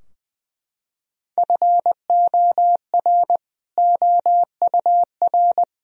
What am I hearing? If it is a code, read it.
Morse code, heard as a keyed tone, about 21 wpm, 700 Hz. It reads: FOR OUR
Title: Sentences from Top 1000 Words - Review Part 3 - 20wpm